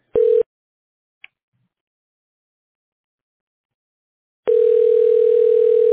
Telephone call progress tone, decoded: ringback tone